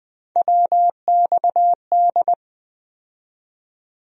Morse code, heard as a keyed tone, about 20 wpm, 700 Hz. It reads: WXD
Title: USA Call Sign Postfixes - 3 Letters 20wpm